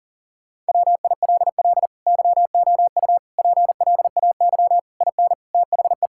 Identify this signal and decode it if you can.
Morse code, heard as a keyed tone, about 40 wpm, 700 Hz. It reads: WILL YOU PLAY IN THE
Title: Sentences from Top 1600 Words 40wpm